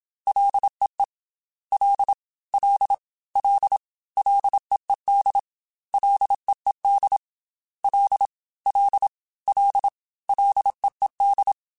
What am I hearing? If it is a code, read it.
Morse code, heard as a keyed tone, about 27 wpm, 800 Hz. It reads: LEE L L L LEED LEED L L L LEED
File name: F and L / F and L mp3 / F and L Prefixes 3x2 Twice @25